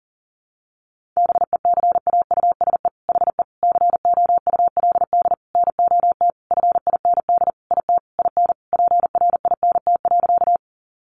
Morse code, heard as a keyed tone, about 40 wpm, 700 Hz. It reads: BECAUSE HE COULD NOT FIND IT IN PRINT.
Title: Sentences from Top 1400 Words 40wpm